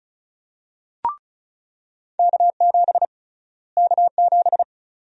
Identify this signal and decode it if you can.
Morse code, heard as a keyed tone, about 34 wpm, 700 Hz. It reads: K7 K7
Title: USA Call Sign Prefixes - Repeated 2x - 35wpm